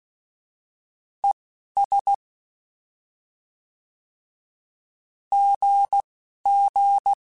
Morse code, heard as a keyed tone, about 16 wpm, 800 Hz. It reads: E S G G
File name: Misc / Misc-mp3s / Words @10